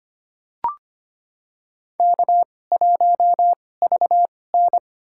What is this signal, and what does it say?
Morse code, heard as a keyed tone, about 25 wpm, spaced as for 18 wpm, 700 Hz. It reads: K1VN
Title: USA Call Signs - 4 Characters - 2x Character Spacing - 25wpm